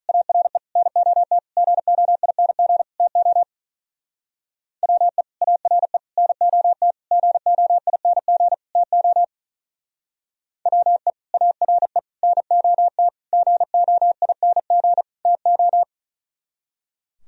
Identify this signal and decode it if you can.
Morse code, heard as a keyed tone, about 42 wpm, 700 Hz. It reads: ARE NOT GOING TO WE ARE NOT GOING TO WE ARE NOT GOING TO
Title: Speed-Racing - Sets of 5 Words 35wpm